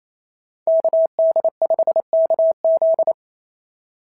Morse code, heard as a keyed tone, about 28 wpm, 690 Hz. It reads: KD5KZ
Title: SOTA QSO Practice 28wpm